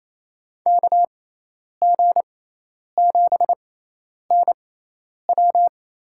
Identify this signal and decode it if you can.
Morse code, heard as a keyed tone, about 28 wpm, spaced as for 10 wpm, 700 Hz. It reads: KG7NW
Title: USA Call Signs - 5 Characters - 6x Character Spacing - 28wpm